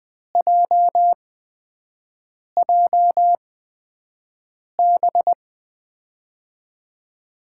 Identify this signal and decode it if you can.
Morse code, heard as a keyed tone, about 21 wpm, spaced as for 5 wpm, 700 Hz. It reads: JJB
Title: USA Call Signs - 6 Characters - 8x Character Spacing - 20wpm